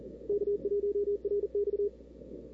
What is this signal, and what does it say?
Morse code, heard as a keyed tone, about 37 wpm, 420 Hz. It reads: K1RX
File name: CW / 2005-k5tr arrl cw dx -03:00